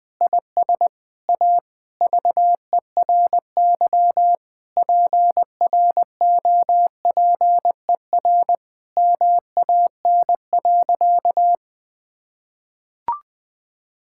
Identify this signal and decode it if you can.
Morse code, heard as a keyed tone, about 20 wpm, 700 Hz. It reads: IS A VERY PROPER MAN.
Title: Sentences from Top 1000 Words - Review Part 5 - 20wpm